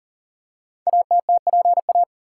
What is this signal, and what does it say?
Morse code, heard as a keyed tone, about 42 wpm, 700 Hz. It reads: ATTPA
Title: Code Groups - 5 Letters - 40wpm